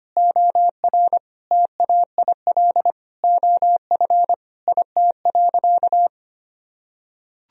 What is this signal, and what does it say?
Morse code, heard as a keyed tone, about 25 wpm, 700 Hz. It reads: OR TAIL OF IT.
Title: Sentences from Top 1000 Words - Review Part 3 - 25wpm